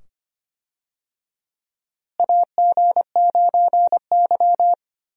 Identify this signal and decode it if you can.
Morse code, heard as a keyed tone, about 25 wpm, 700 Hz. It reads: AG9Y
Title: USA Call Signs - 4 Characters - Repeated 2x - 25wpm